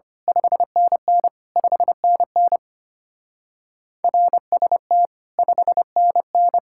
Morse code, heard as a keyed tone, about 29 wpm, 700 Hz. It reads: 5NN 5NN RST 5NN
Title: Speed-Racing Common QSO Phrases - 25wpm